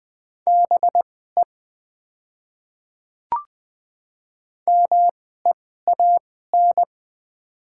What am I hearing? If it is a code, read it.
Morse code, heard as a keyed tone, about 20 wpm, spaced as for 14 wpm, 700 Hz. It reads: BE MEAN
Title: 2x Character Spacing - Top 200 Words - 20wpm